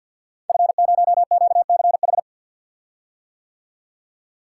Morse code, heard as a keyed tone, about 51 wpm, 700 Hz. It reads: F0QCH
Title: Code Groups - 5 Letter:Numbers - 50wpm